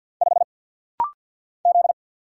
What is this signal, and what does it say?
Morse code, heard as a keyed tone, about 49 wpm, 700 Hz. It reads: 5 Z
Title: Single Letter-Number Alt - Rapid-Fire 50wpm